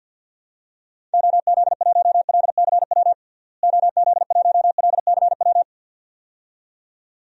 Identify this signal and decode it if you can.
Morse code, heard as a keyed tone, about 53 wpm, 700 Hz. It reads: OZ1LCW OZ1LCW
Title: DXCC-100 Call Signs - 6 Characters - Repeated 2x - 50wpm